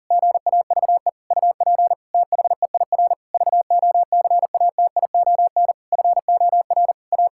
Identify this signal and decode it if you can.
Morse code, heard as a keyed tone, about 37 wpm, 700 Hz. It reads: GAVE UP THEIR VOCATION FOR A
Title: Pangram Challenge v2 - 40wpm